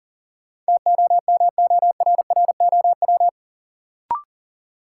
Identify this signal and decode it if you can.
Morse code, heard as a keyed tone, about 40 wpm, 700 Hz. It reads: TOMORROW
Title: Sentences from Top 1100 Words - New Word Drill - 40wpm